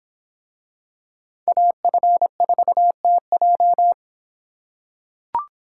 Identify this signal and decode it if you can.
Morse code, heard as a keyed tone, about 26 wpm, 700 Hz. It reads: AF4TJ